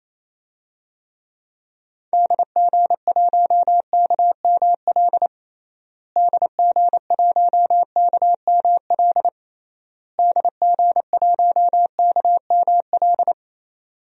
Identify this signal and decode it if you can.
Morse code, heard as a keyed tone, about 28 wpm, 700 Hz. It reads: DG1KML DG1KML DG1KML
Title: DXCC-100 Call Signs - 6 Characters - Repeated 3x - 28wpm